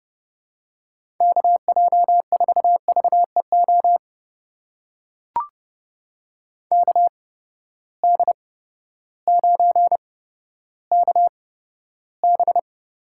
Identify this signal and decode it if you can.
Morse code, heard as a keyed tone, about 30 wpm, spaced as for 14 wpm, 700 Hz. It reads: KJ4VEO KD9KB
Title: USA Call Signs - 8x Character Spacing - 30wpm